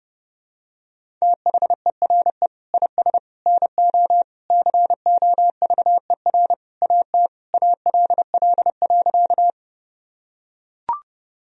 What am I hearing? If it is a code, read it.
Morse code, heard as a keyed tone, about 32 wpm, 700 Hz. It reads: THERE IS NO COVER AT ALL.
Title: Sentences from Top 1000 Words - Review Part 1 - 30wpm